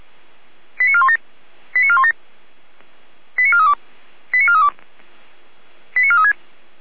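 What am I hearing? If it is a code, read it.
Five-tone selective call (ZVEI), heard as quick runs of five tones, tones of about 70 ms. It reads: 89417 89417 89432 89432 89436
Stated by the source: ZVEI five-tone selective call received on an SDR.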